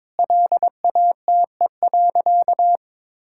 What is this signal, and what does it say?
Morse code, heard as a keyed tone, about 22 wpm, 700 Hz. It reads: LATE.
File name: practice / 22wpm / Sentences with 4-Word Phrases - 22wpm